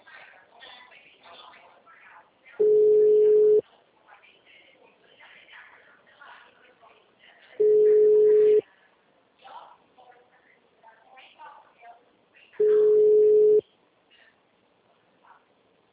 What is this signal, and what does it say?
Ringback tone (Europe, 425 Hz), with speech.
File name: out- -2009- - - .39637